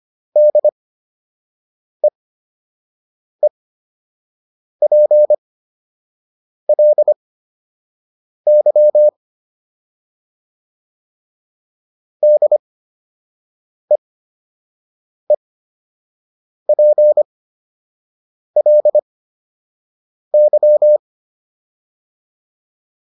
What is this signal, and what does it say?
Morse code, heard as a keyed tone, about 25 wpm, spaced as for 6 wpm, 600 Hz. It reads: DEEPLY DEEPLY